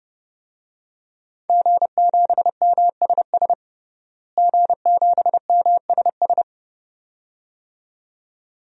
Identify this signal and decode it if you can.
Morse code, heard as a keyed tone, about 32 wpm, 700 Hz. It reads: G7MSS G7MSS
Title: DXCC-100 Call Signs - 5 Characters - Repeated 2x - 30wpm